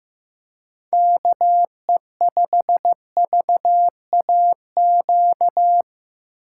Morse code, heard as a keyed tone, about 15 wpm, 700 Hz. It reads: KE5VAQ
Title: USA Call Signs - 6 Characters - 8x Character Spacing - 15wpm